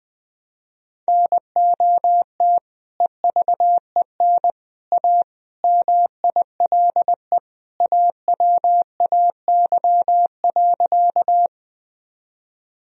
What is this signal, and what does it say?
Morse code, heard as a keyed tone, about 20 wpm, 700 Hz. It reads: NOT EVEN A MILE AWAY.
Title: Sentences from Top 1000 Words - Review Part 1 - 20wpm